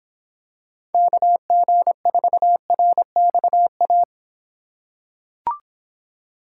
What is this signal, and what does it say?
Morse code, heard as a keyed tone, about 26 wpm, 700 Hz. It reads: KG4RXA